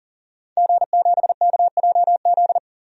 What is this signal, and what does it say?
Morse code, heard as a keyed tone, about 40 wpm, 700 Hz. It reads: G7KJZ
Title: DXCC-100 Call Signs - 5 Characters - 40wpm